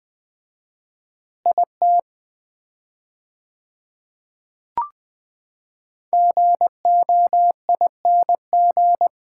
Morse code, heard as a keyed tone, about 21 wpm, 700 Hz. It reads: IT GOING
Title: 4x Word-Spacing - Sets of 5 Words - 20wpm